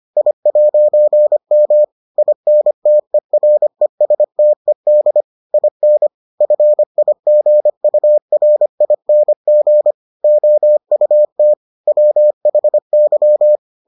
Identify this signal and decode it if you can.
Morse code, heard as a keyed tone, about 25 wpm, 600 Hz. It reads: I'M INTERESTED IN FIGURING OUT WHY